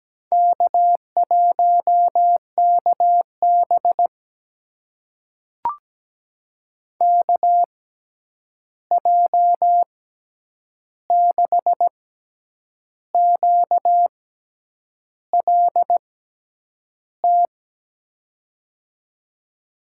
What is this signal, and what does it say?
Morse code, heard as a keyed tone, about 17 wpm, spaced as for 8 wpm, 700 Hz. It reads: K1KB KJ6QLT
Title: USA Call Signs - 6x Character Spacing - 17wpm